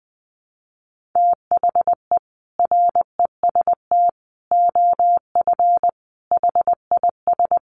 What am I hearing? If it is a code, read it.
Morse code, heard as a keyed tone, about 20 wpm, 700 Hz. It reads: THE REST OF HIS
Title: Sets of 5 Words 20wpm